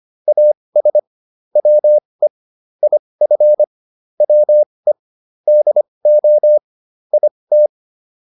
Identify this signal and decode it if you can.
Morse code, heard as a keyed tone, about 25 wpm, spaced as for 20 wpm, 600 Hz. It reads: AS WE IF WE DO IT
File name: WD101a 20